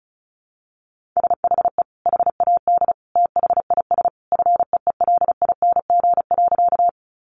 Morse code, heard as a keyed tone, about 34 wpm, 700 Hz. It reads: SHE HAD THIS FEELING.
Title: Sentences from Top 1200 Words 35wpm